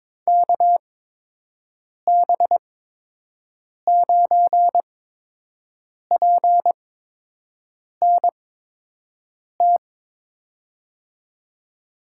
Morse code, heard as a keyed tone, about 22 wpm, spaced as for 6 wpm, 700 Hz. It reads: KB9PNT
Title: USA Call Signs - 8x Character Spacing - 22wpm